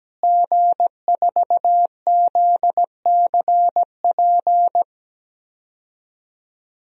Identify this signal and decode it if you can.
Morse code, heard as a keyed tone, about 17 wpm, 700 Hz. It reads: G4ZCP
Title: DXCC-100 Call Signs - 5 Characters - Repeated 2x - 17wpm